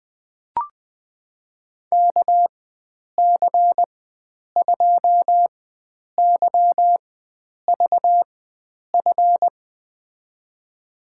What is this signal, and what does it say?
Morse code, heard as a keyed tone, about 21 wpm, spaced as for 9 wpm, 700 Hz. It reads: KC2YVF
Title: USA Call Signs - 6 Characters - 4x Character Spacing - 20wpm